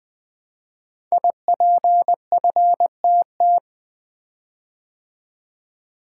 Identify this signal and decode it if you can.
Morse code, heard as a keyed tone, about 20 wpm, 700 Hz. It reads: IPFTT
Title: Code Groups - 5 Letters - 20wpm